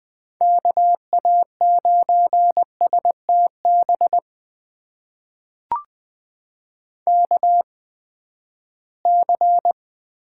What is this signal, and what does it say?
Morse code, heard as a keyed tone, about 20 wpm, spaced as for 14 wpm, 700 Hz. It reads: KA9STB KC
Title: USA Call Signs - 6 Characters - 8x Character Spacing - 20wpm